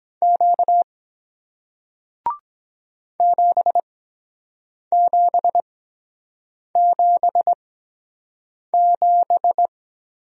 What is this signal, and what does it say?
Morse code, heard as a keyed tone, about 23 wpm, spaced as for 7 wpm, 700 Hz. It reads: Q 7777